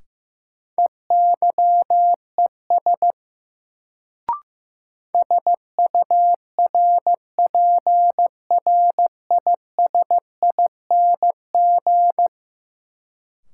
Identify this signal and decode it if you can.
Morse code, heard as a keyed tone, about 15 wpm, 700 Hz. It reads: EYES SURPRISING
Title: Sets of 5 Words - New Word Drill - 15wpm